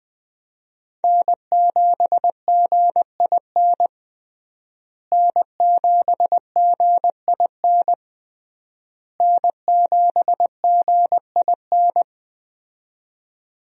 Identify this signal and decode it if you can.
Morse code, heard as a keyed tone, about 21 wpm, 700 Hz. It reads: N7GIN N7GIN N7GIN
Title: USA Call Signs - 5 Characters - Repeated 3x - 20wpm